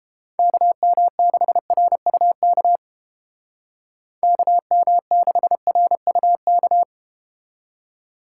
Morse code, heard as a keyed tone, about 32 wpm, 700 Hz. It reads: KM6RUK KM6RUK